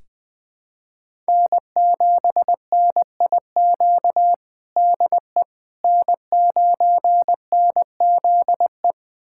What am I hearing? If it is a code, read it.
Morse code, heard as a keyed tone, about 19 wpm, 700 Hz. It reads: N7NIQ DE N9NZE
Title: USA Callsign DE Callsign - 5 Characters - 20wpm